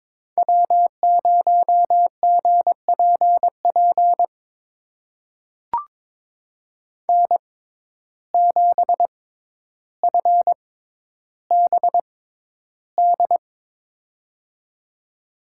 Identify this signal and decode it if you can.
Morse code, heard as a keyed tone, about 22 wpm, spaced as for 11 wpm, 700 Hz. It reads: W0GPP N7FBD
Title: USA Call Signs - 5 Characters - 6x Character Spacing - 22wpm